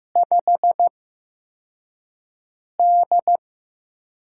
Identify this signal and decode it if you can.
Morse code, heard as a keyed tone, about 15 wpm, spaced as for 4 wpm, 700 Hz. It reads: 5D
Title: USA Call Signs - 4 Characters - 8x Character Spacing - 15wpm